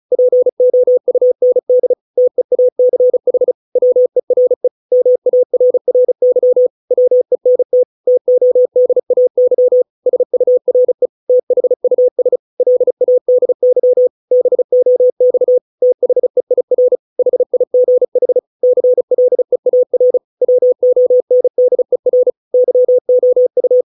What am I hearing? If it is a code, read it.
Morse code, heard as a keyed tone, about 34 wpm, 500 Hz. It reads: POUND TEACH WERE MARRY WENT TODAY SURE THUS LADY BOX THEIR HIGH CLEAR WONDER YOU